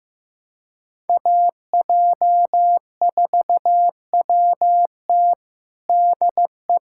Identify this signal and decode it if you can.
Morse code, heard as a keyed tone, about 15 wpm, 700 Hz. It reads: AJ4WT DE
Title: USA Callsign DE Callsign - 5 Characters - 15wpm